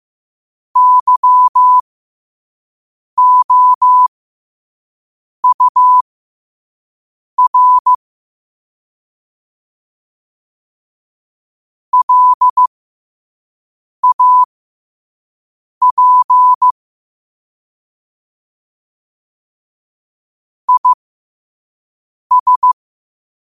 Morse code, heard as a keyed tone, about 15 wpm, spaced as for 5 wpm, 1000 Hz. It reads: YOUR LAP IS
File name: qotdfiles / 5wpm